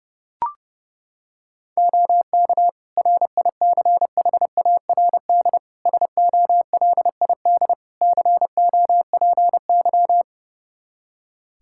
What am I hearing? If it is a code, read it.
Morse code, heard as a keyed tone, about 32 wpm, 700 Hz. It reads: OK RICHARD SOLID COPY
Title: Practice QSO - Exchange 3 - More Info - 30wpm